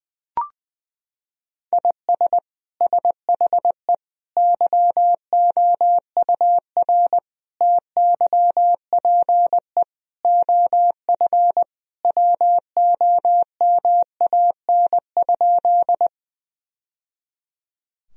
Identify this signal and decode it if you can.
Morse code, heard as a keyed tone, about 20 wpm, 700 Hz. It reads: IS SHE YOUR TYPE OF WOMAN?
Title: Sentences from Top 1000 Words - Review Part 3 - 20wpm